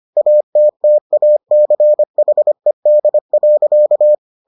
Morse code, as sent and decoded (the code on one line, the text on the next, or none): .- - - .- -.-. .... -.. .-.-.-
ATTACHED.